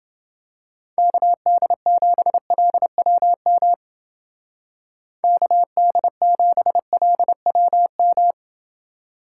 Morse code, heard as a keyed tone, about 29 wpm, 700 Hz. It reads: KD7LWM KD7LWM